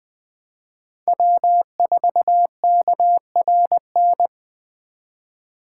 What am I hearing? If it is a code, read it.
Morse code, heard as a keyed tone, about 20 wpm, 700 Hz. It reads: W4KRN
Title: 2023 SOTA Chaser Call Signs - 20wpm